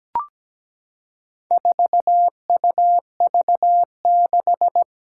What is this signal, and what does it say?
Morse code, heard as a keyed tone, about 17 wpm, 700 Hz. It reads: 4UV6
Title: Four Challenging Letter-Number Combinations 17wpm